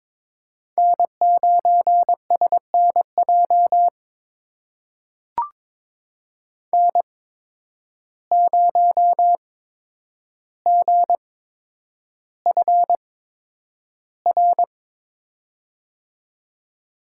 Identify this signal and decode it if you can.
Morse code, heard as a keyed tone, about 22 wpm, spaced as for 9 wpm, 700 Hz. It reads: N9SNJ N0GFR
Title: USA Call Signs - 5 Characters - 8x Character Spacing - 22wpm